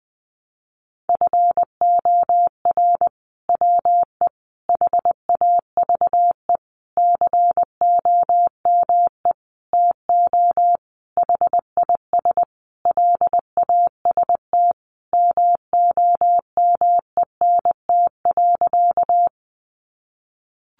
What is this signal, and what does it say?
Morse code, heard as a keyed tone, about 20 wpm, 700 Hz. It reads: FOR WE HAVE COME TO HIS LAST MOMENT.
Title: Sentences from Top 1000 Words 20wpm